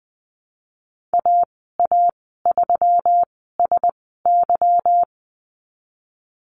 Morse code, heard as a keyed tone, about 20 wpm, spaced as for 14 wpm, 700 Hz. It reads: AA3SY
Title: 173 Lesson 28g - Number 9 - Call Signs 20wpm